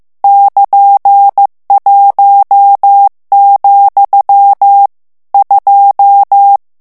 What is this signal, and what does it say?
Morse code, heard as a keyed tone, about 15 wpm, 800 Hz. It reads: <KN>1, 2
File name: morse mp3 / ITALIEN texte clair 2 vitesse 14 mots minute